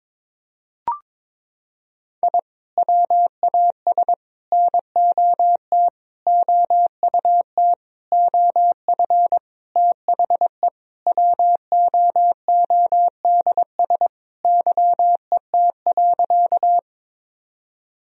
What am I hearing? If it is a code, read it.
Morse code, heard as a keyed tone, about 22 wpm, 700 Hz. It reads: I WAS NOT OUT OF THE WOODS YET.
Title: Sentences from Top 1000 Words - Review Part 2 - 22wpm